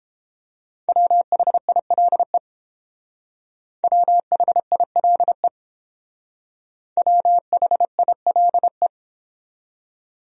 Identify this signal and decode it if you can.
Morse code, heard as a keyed tone, about 31 wpm, 700 Hz. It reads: WHILE WHILE WHILE